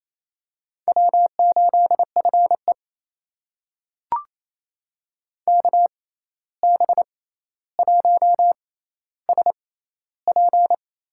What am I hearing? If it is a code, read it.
Morse code, heard as a keyed tone, about 28 wpm, spaced as for 13 wpm, 700 Hz. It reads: W8FE KB1SP
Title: USA Call Signs - 6x Character Spacing - 28wpm